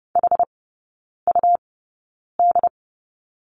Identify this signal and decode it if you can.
Morse code, heard as a keyed tone, about 30 wpm, spaced as for 9 wpm, 700 Hz. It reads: HUD
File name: Lesson 19e - Letter w - Three Characters 30wpm